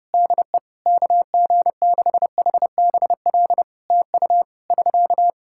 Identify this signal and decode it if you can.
Morse code, heard as a keyed tone, about 30 wpm, 700 Hz. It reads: DE KG6HBL TU <SK>
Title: Practice QSO - Exchange 4 - Ending - 30wpm